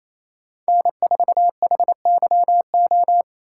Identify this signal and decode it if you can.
Morse code, heard as a keyed tone, about 35 wpm, 700 Hz. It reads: N4HYO